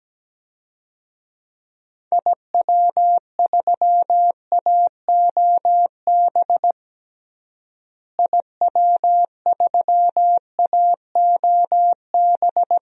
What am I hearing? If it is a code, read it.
Morse code, heard as a keyed tone, about 17 wpm, 700 Hz. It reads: IW3AOB IW3AOB
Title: DXCC-100 Call Signs - 6 Characters - Repeated 2x - 17wpm